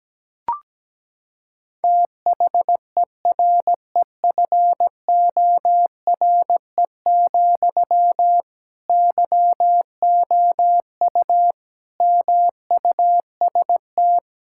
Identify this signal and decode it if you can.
Morse code, heard as a keyed tone, about 17 wpm, 700 Hz. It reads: THEREFORE, YOU MUST
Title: Sentences from Top 1000 Words - Review Part 5 - 17wpm